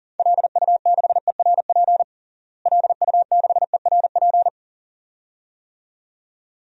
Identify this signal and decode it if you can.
Morse code, heard as a keyed tone, about 37 wpm, 700 Hz. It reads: LU6ERP LU6ERP
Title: DXCC-100 Call Signs - 6 Characters - Repeated 2x - 40wpm